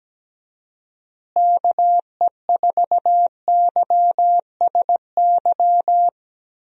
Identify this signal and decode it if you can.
Morse code, heard as a keyed tone, about 17 wpm, 700 Hz. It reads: KE4YSY
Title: USA Call Signs - 6 Characters - Repeated 2x - 17wpm